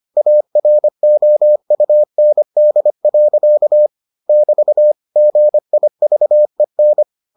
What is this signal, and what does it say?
Morse code, heard as a keyed tone, about 25 wpm, 600 Hz. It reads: AROUND. <BT> GIVEN